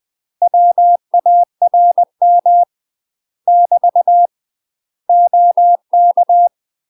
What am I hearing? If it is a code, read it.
Morse code, heard as a keyed tone, about 20 wpm, 700 Hz. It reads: WARM <BT> OK
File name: UK MASTER QSOs TRAINING / 20WPM / UK MASTER QSOs 0061